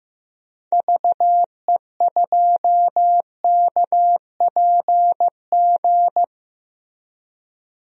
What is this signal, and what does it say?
Morse code, heard as a keyed tone, about 15 wpm, 700 Hz. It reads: VE2KPG
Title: DXCC-100 Call Signs - 6 Characters - 15wpm